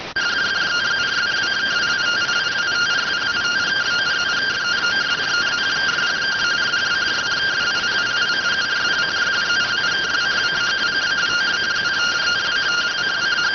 Two-tone teleprinter signal, about 170 Hz shift.